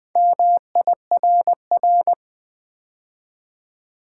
Morse code, heard as a keyed tone, about 20 wpm, 700 Hz. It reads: MIRR
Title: Build A Word - 2 Letters At A Time - 6-Letter Words - 20wpm